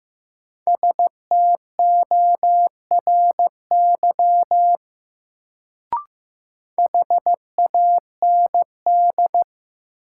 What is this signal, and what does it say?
Morse code, heard as a keyed tone, about 15 wpm, 700 Hz. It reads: STORY HAND
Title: Sets of 3 Words - New Word Drill - 15wpm